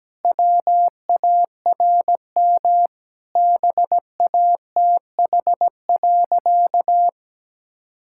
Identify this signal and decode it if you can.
Morse code, heard as a keyed tone, about 17 wpm, 700 Hz. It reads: WARM BATH.
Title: Sentences from Top 1600 Words 17wpm